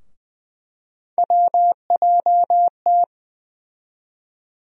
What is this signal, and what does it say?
Morse code, heard as a keyed tone, about 20 wpm, 700 Hz. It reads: WJT
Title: USA Call Sign Postfixes - 3 Letters 20wpm